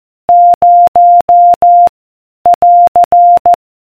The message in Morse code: 0 <AR>